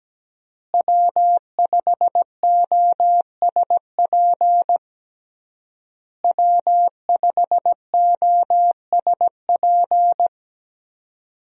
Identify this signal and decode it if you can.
Morse code, heard as a keyed tone, about 17 wpm, 700 Hz. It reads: W5OSP W5OSP
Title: USA Call Signs - 5 Characters - Repeated 3x - 17wpm